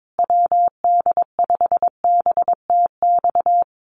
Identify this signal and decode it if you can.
Morse code, heard as a keyed tone, about 22 wpm, 700 Hz. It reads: WD5BTX
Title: Common QSO Phrases Encore 22wpm